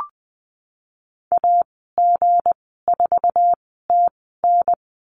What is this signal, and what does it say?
Morse code, heard as a keyed tone, about 20 wpm, spaced as for 15 wpm, 700 Hz. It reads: AG4TN
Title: 167 Lesson 27g - Number 7 - Call Signs 20wpm